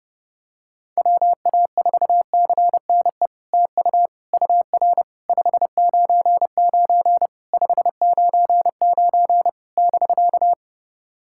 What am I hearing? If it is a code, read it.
Morse code, heard as a keyed tone, about 30 wpm, 690 Hz. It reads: WA4CNE TU UR 599 599 <BK>
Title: SOTA QSO Practice 30wpm